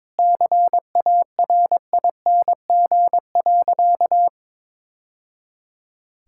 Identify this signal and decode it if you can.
Morse code, heard as a keyed tone, about 22 wpm, 700 Hz. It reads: CARING.
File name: practice / 22wpm / Sentences with 2-Word Phrases - 22wpm